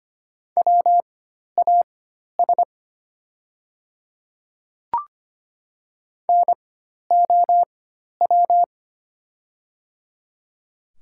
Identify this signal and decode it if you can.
Morse code, heard as a keyed tone, about 25 wpm, spaced as for 12 wpm, 700 Hz. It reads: WAS NOW
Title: 4x Character Spacing - Top 100 Words - 25wpm